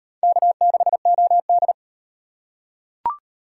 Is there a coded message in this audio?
Yes: Morse code from 0.2 to 1.7 s, heard as a keyed tone, about 38 wpm, 700 Hz. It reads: K6OD